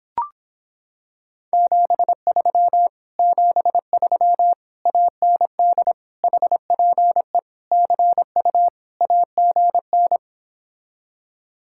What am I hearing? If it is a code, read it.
Morse code, heard as a keyed tone, about 26 wpm, 700 Hz. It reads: 73 73 AND HPE CU AGN